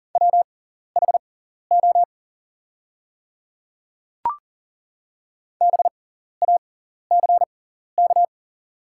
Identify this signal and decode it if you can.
Morse code, heard as a keyed tone, about 40 wpm, spaced as for 14 wpm, 700 Hz. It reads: WHO BACK
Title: 6x Character Spacing - Top 300 Words - 40wpm